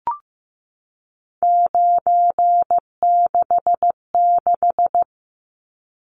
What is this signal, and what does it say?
Morse code, heard as a keyed tone, about 15 wpm, 700 Hz. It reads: 966
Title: Numbers - 3-Digit 15wpm